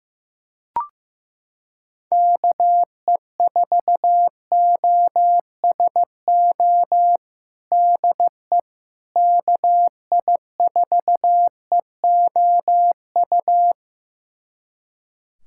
Morse code, heard as a keyed tone, about 16 wpm, 700 Hz. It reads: KE4OSO DE KI4EOU
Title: USA Callsign DE Callsign - 6 Characters - 15wpm